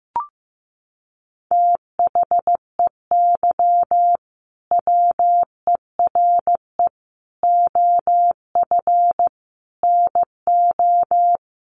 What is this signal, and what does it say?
Morse code, heard as a keyed tone, about 15 wpm, 700 Hz. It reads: THEY WERE OF NO
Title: Sentences from Top 1200 Words 15wpm